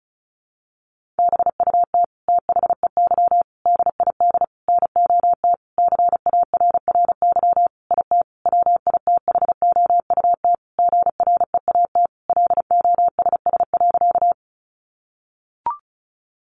Morse code, heard as a keyed tone, about 34 wpm, 700 Hz. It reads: BUT THEY DID NOT CARRY IT WITHOUT GREAT LOSS.
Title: Sentences from Top 1400 Words 35wpm